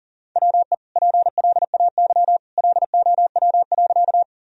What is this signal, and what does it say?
Morse code, heard as a keyed tone, about 40 wpm, 700 Hz. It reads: WE PLAY LOW.
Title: Sentences from Top 1000 Words - Review Part 1 - 40wpm